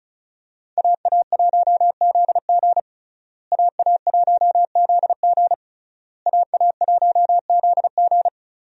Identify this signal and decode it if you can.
Morse code, heard as a keyed tone, about 36 wpm, 700 Hz. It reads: AA1ZG AA1ZG AA1ZG
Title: USA Call Signs - 5 Characters - Repeated 3x - 35wpm